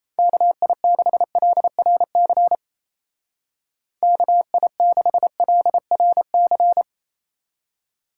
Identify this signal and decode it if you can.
Morse code, heard as a keyed tone, about 35 wpm, 700 Hz. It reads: KI6LRC KI6LRC